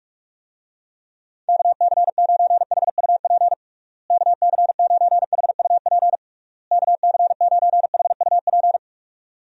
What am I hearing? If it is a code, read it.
Morse code, heard as a keyed tone, about 46 wpm, 700 Hz. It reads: KC9HUP KC9HUP KC9HUP
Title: USA Call Signs - 6 Characters - Repeated 3x - 45wpm